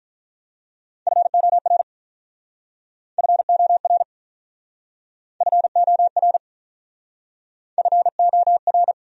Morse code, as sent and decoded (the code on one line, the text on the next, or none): ..-. --- .-. / ..-. --- .-. / ..-. --- .-. / ..-. --- .-.
FOR FOR FOR FOR